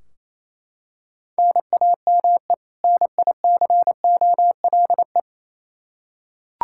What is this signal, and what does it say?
Morse code, heard as a keyed tone, about 28 wpm, 700 Hz. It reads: NAME NICOLE
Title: Practice QSO - Exchange 1 - Beginning - 28wpm